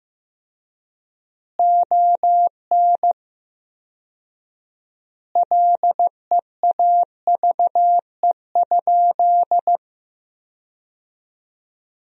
Morse code, heard as a keyed tone, about 15 wpm, 700 Hz. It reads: ON LEAVE?
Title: 4x Word-Spacing - Sentences from Top 400 Words - 15wpm